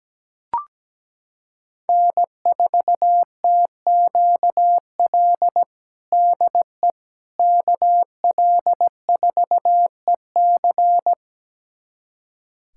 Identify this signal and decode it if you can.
Morse code, heard as a keyed tone, about 17 wpm, 700 Hz. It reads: N4TQL DE KL4EC
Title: USA Callsign DE Callsign - 5 Characters - 17wpm